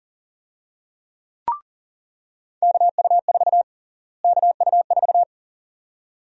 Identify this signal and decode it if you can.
Morse code, heard as a keyed tone, about 40 wpm, 700 Hz. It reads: KU4 KU4
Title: USA Call Sign Prefixes - Repeated 2x - 40wpm